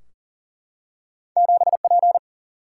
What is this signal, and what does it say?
Morse code, heard as a keyed tone, about 39 wpm, 700 Hz. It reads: 7P
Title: Single-Letter Single-Number 40wpm